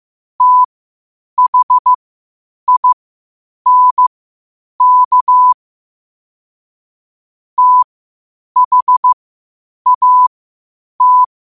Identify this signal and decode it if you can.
Morse code, heard as a keyed tone, about 15 wpm, spaced as for 8 wpm, 1000 Hz. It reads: THINK THAT
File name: qotdfiles / 8wpm